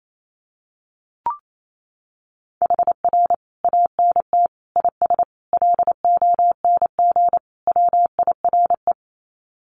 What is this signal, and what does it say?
Morse code, heard as a keyed tone, about 28 wpm, 700 Hz. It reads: HR ANT IS LONG WIRE
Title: Common QSO Phrases 28wpm